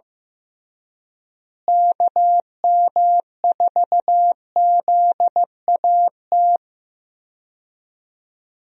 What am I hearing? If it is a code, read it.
Morse code, heard as a keyed tone, about 15 wpm, 700 Hz. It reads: KM4ZAT
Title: USA Call Signs - 6 Characters - Repeated 3x - 15wpm